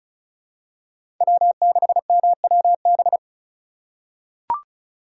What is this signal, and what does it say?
Morse code, heard as a keyed tone, about 36 wpm, 700 Hz. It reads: W6MWB
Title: USA Call Signs - 5 Characters - 2x Character Spacing - 35wpm